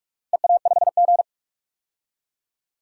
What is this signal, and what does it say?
Morse code, heard as a keyed tone, about 42 wpm, 700 Hz. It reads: EA5G
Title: DXCC-100 Call Signs - 4 Characters - 45wpm